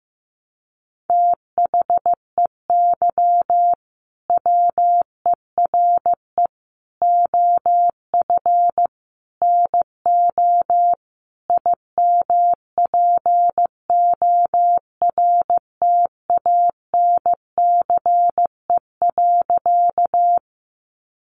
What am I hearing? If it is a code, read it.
Morse code, heard as a keyed tone, about 14 wpm, 700 Hz. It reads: THEY WERE OF NO IMPORTANCE.
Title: Sentences from Top 1200 Words 15wpm